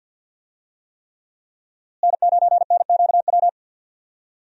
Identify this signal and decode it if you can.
Morse code, heard as a keyed tone, about 48 wpm, 700 Hz. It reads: N9NQW
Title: USA Call Signs - 5 Characters - Repeated 3x - 50wpm